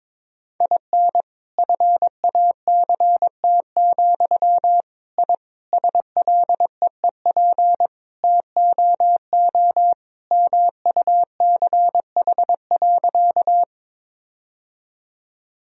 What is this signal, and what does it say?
Morse code, heard as a keyed tone, about 22 wpm, 700 Hz. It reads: IN FACT, I SLEEP TOO MUCH.
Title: Sentences from Top 1000 Words - Review Part 3 - 22wpm